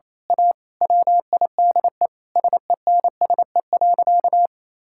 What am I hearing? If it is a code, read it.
Morse code, heard as a keyed tone, about 28 wpm, 700 Hz. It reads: A WIDE SENSE.
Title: Sentences from Top 1000 Words - Review Part 6 - 28wpm